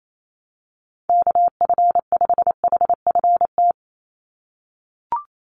Morse code, heard as a keyed tone, about 28 wpm, 700 Hz. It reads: KF5HFT
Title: USA Call Signs 28wpm